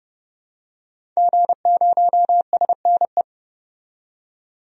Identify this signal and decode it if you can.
Morse code, heard as a keyed tone, about 31 wpm, 700 Hz. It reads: G0SNE
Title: DXCC-100 Call Signs - 5 Characters - 30wpm